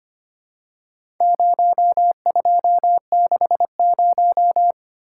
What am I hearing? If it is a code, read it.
Morse code, heard as a keyed tone, about 25 wpm, 700 Hz. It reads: 0260
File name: practice / 25wpm / Four Letter-Number Combinations - 25wpm